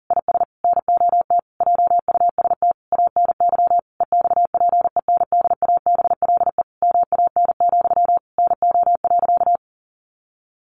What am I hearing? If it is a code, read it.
Morse code, heard as a keyed tone, about 40 wpm, 700 Hz. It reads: IS NOT JUST ANY EXPENDABLE MAN, NO.
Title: A Sound Of Thunder 40wpm